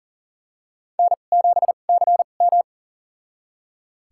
Morse code, heard as a keyed tone, about 40 wpm, spaced as for 29 wpm, 700 Hz. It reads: N7CM
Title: USA Call Signs - 4 Characters - 2x Character Spacing - 40wpm